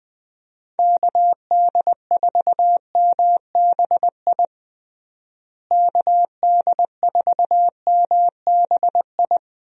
Morse code, heard as a keyed tone, about 20 wpm, 700 Hz. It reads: KD4MBI KD4MBI
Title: USA Call Signs - 6 Characters - Repeated 2x - 20wpm